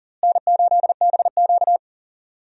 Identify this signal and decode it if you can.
Morse code, heard as a keyed tone, about 40 wpm, 700 Hz. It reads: N8BQ
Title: USA Call Signs - 4 Characters - Repeated 3x - 40wpm